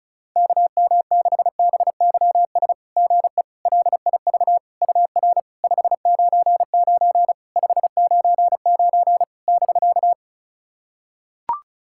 Morse code, heard as a keyed tone, about 36 wpm, 700 Hz. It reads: KM6BYS GE LIV UR 599 599 <BK>
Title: SOTA QSO Practice 35wpm